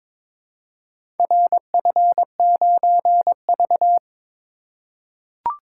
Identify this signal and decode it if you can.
Morse code, heard as a keyed tone, about 22 wpm, 700 Hz. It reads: RF9V
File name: practice / 22wpm / Four Letter-Number Combinations - 22wpm